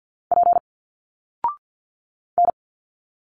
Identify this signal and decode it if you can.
Morse code, heard as a keyed tone, about 51 wpm, 700 Hz. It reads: P N
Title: Single Letters 50wpm